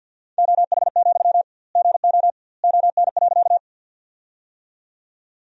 Morse code, heard as a keyed tone, about 51 wpm, 700 Hz. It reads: OH, GO ON.
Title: 2x Word-Spacing - Sentences from Top 500 Words - 50wpm